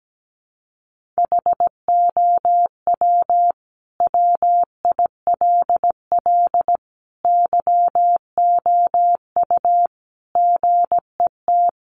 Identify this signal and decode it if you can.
Morse code, heard as a keyed tone, about 17 wpm, 700 Hz. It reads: HOW WILL YOU GET
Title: Sentences from Top 1500 Words 17wpm